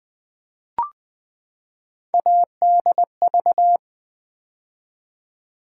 Morse code, heard as a keyed tone, about 20 wpm, 700 Hz. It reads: ADV
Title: Build A Word - 3 Letters At A Time - 9-Letter Words - 20wpm